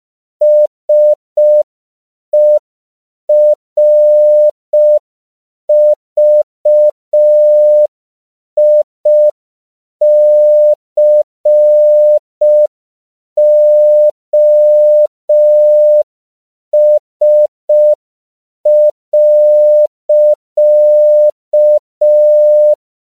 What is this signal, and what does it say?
Morse code, heard as a keyed tone, about 5 wpm, 600 Hz. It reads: SERVICOS.
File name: CW / 5PPM